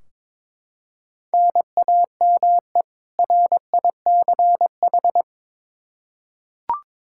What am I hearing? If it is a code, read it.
Morse code, heard as a keyed tone, about 22 wpm, 700 Hz. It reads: NAME RICH
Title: K1USN SST QSO Practice 22wpm